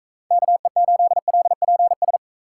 Morse code, heard as a keyed tone, about 42 wpm, 700 Hz. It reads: KE8LPS